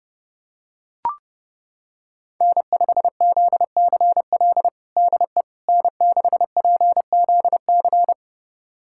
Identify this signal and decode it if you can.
Morse code, heard as a keyed tone, about 30 wpm, 700 Hz. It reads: N5ZCL DE N6PZC
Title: USA Callsign DE Callsign 30wpm